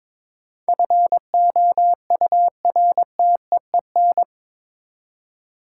Morse code, heard as a keyed tone, about 22 wpm, 700 Hz. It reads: FOURTEEN
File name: practice / 22wpm / Eight Letter Words - 22wpm